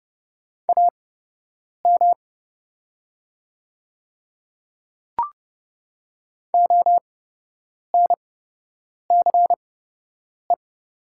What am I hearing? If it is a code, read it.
Morse code, heard as a keyed tone, about 30 wpm, spaced as for 8 wpm, 700 Hz. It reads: AM ONCE
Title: 8x Character Spacing - Top 500 Words - 30wpm